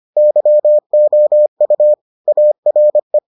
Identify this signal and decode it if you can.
Morse code, heard as a keyed tone, about 25 wpm, 600 Hz. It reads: YOU ARE